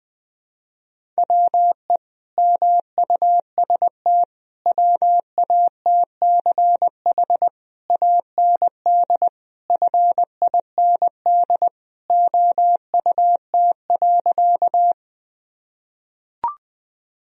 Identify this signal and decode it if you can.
Morse code, heard as a keyed tone, about 19 wpm, 700 Hz. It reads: WE MUST WATCH AND FIND OUT.
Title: Sentences from Top 1000 Words - Review Part 1 - 20wpm